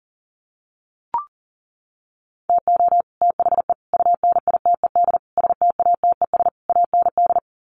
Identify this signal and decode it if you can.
Morse code, heard as a keyed tone, about 37 wpm, 700 Hz. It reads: TO THE UNITED STATES AND
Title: Sets of 5 Words Encore 40wpm